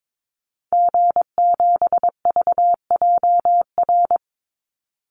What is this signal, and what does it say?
Morse code, heard as a keyed tone, about 22 wpm, 700 Hz. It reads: G7VJR
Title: Call Signs Repeated 2x - 22wpm